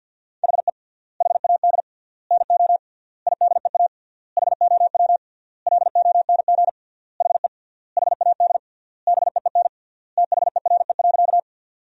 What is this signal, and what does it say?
Morse code, heard as a keyed tone, about 50 wpm, 700 Hz. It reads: HE HAD NO IDEA HOW LONG HE HAD BEEN THERE.
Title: 3x Word-Spacing - Sentences from Top 300 Words - 50wpm